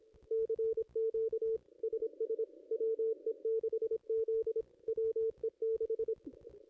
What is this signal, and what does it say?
Morse code, heard as a keyed tone, about 25 wpm, 440 Hz. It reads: CQ SS WE6Z WE6
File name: contests / ss-cw-k5tr-op-n5rz -1700